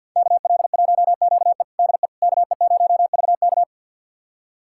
Morse code, heard as a keyed tone, about 52 wpm, 700 Hz. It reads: KL1QE DE KE0VK
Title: USA Callsign DE Callsign - 5 Characters - 50wpm